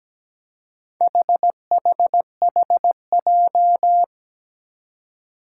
Morse code, heard as a keyed tone, about 17 wpm, 700 Hz. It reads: HHHJ
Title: Four Challenging Letter-Number Combinations 17wpm